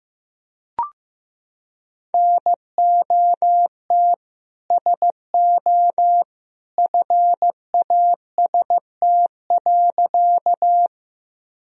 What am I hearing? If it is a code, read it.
Morse code, heard as a keyed tone, about 15 wpm, 700 Hz. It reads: NOT SO FAST.
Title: Sentences from Top 1000 Words - Review Part 2 - 15wpm